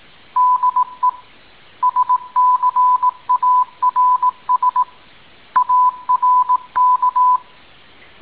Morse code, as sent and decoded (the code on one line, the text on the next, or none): -.. / ... -.-. .- .-. ... / .- .-. -.-
DE SCARS ARK